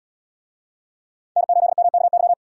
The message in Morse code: N8MKY

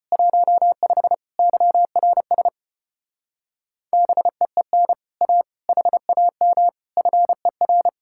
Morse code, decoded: 15 YRS BEEN A HAM FER

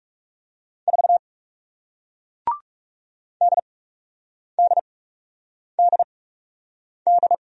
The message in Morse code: 4 DDDD

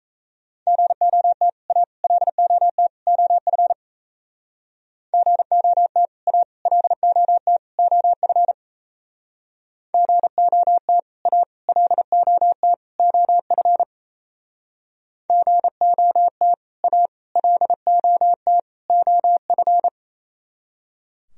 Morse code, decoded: GOT A LOT OF GOT A LOT OF GOT A LOT OF GOT A LOT OF

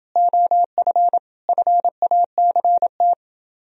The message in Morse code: OF FACT

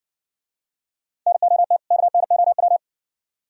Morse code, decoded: NOT KNOW

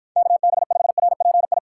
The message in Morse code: KB5RPI

